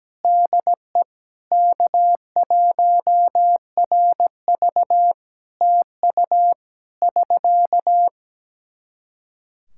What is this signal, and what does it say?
Morse code, heard as a keyed tone, about 17 wpm, 700 Hz. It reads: DE K1RV TU <SK>
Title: Practice QSO - Exchange 4 - Ending - 17wpm